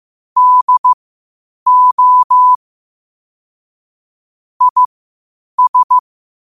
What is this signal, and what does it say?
Morse code, heard as a keyed tone, about 15 wpm, spaced as for 8 wpm, 1000 Hz. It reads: DO IS
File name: qotdfiles / 8wpm